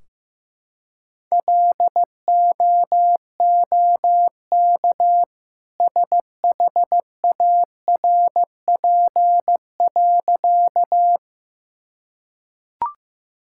Morse code, heard as a keyed tone, about 15 wpm, 700 Hz. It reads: LOOK SHARP.
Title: Sentences from Top 1600 Words 15wpm